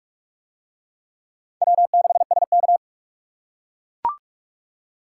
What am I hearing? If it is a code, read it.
Morse code, heard as a keyed tone, about 44 wpm, 700 Hz. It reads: W6SK